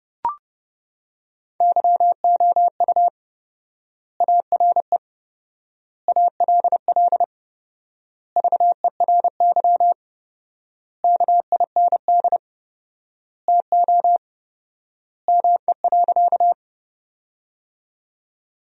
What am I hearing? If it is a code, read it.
Morse code, heard as a keyed tone, about 30 wpm, 700 Hz. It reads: YOU ARE ALL VERY KIND TO ME.
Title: 4x Word-Spacing - Sentences from Top 200 Words - 30wpm